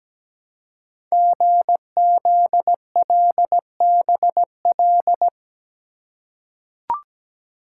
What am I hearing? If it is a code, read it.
Morse code, heard as a keyed tone, about 17 wpm, 700 Hz. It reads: GZLBL
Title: Code Groups - 5 Letters - 17wpm